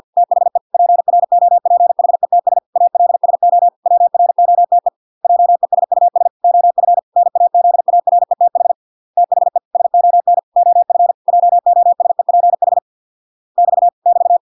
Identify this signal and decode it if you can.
Morse code, heard as a keyed tone, about 50 wpm, 700 Hz. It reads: THE PROPHETS ALSO WROTE JESUS OF NAZARETH THE SON OF JOSEPH <BT> <BT>